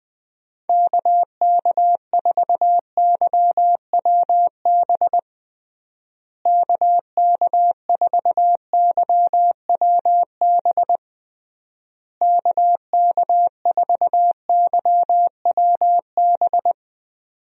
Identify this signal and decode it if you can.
Morse code, heard as a keyed tone, about 20 wpm, 700 Hz. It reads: KK4YWB KK4YWB KK4YWB
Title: USA Call Signs - 6 Characters - Repeated 3x - 20wpm